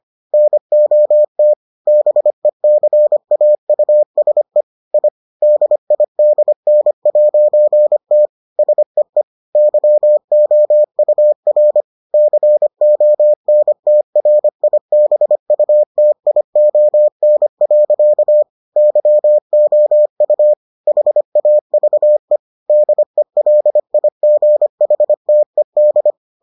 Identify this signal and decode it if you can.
Morse code, heard as a keyed tone, about 25 wpm, 600 Hz. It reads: NOT BECAUSE I DIDN'T SEE YOUR CONTRIBUTION. YOU HAVE DELIGHTED